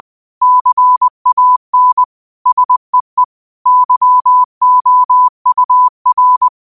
Morse code, heard as a keyed tone, about 20 wpm, 1000 Hz. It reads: CAN SEE YOUR
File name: qotdfiles / 20wpm